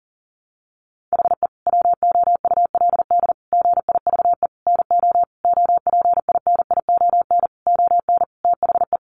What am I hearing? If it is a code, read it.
Morse code, heard as a keyed tone, about 40 wpm, 700 Hz. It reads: HE WOULD GIVE NO OPINION ON THE
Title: Sentences from Top 1200 Words 40wpm